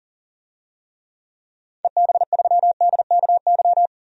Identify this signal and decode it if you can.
Morse code, heard as a keyed tone, about 40 wpm, 700 Hz. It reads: EB3DKY
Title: DXCC-100 Call Signs - 6 Characters - 40wpm